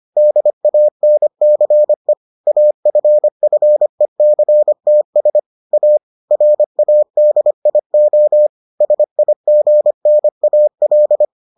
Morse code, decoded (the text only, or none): DANCE AFFECTS A RADIO SIGNAL